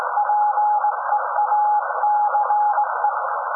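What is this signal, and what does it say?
Morse code, heard as a keyed tone, about 17 wpm, 840 Hz, with noise. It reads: PEACE
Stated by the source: SAQ Grimeton Radio, Alexanderson 200 kW alternator on 17.2 kHz, received in Potsdam with a very strong signal on World Radio Day.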